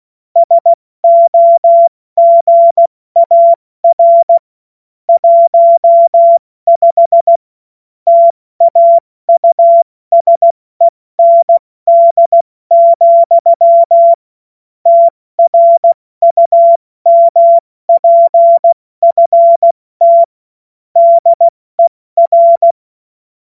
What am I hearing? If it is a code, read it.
Morse code, heard as a keyed tone, about 16 wpm, 680 Hz. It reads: SOGAR 15 TAUSEND, TRUMPFT DER
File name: Witz-14-Oktober-2025-70-F80